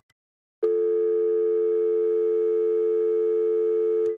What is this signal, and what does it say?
Dial tone (North America), 350 + 440 Hz.
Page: Office Telephone Speakerphone Dial Tone